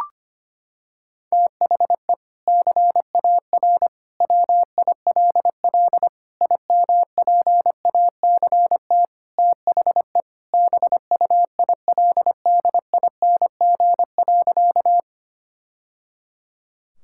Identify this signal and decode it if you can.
Morse code, heard as a keyed tone, about 25 wpm, 700 Hz. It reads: THE CAR WILL IMPACT THE BUILDING.
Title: Sentences from Top 1000 Words - Review Part 6 - 25wpm